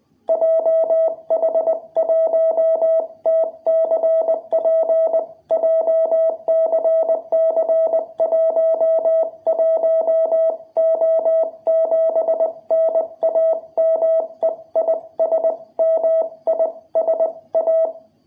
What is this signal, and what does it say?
Morse code, heard as a keyed tone, about 20 wpm, 660 Hz. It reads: JH1TCP JCC11O7 NAME IS MISA
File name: jh8cbh / 2021cw